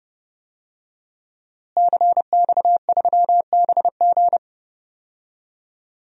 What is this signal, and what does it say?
Morse code, heard as a keyed tone, about 32 wpm, 700 Hz. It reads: CX3BG
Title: DXCC-100 Call Signs - 5 Characters - Repeated 2x - 30wpm